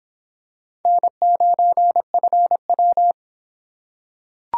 Morse code, heard as a keyed tone, about 26 wpm, 700 Hz. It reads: N9FW